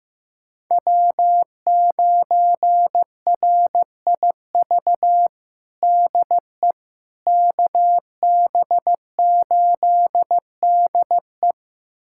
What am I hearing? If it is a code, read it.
Morse code, heard as a keyed tone, about 15 wpm, 700 Hz. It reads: W9RIV DE KB8DE
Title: USA Callsign DE Callsign - 5 Characters - 15wpm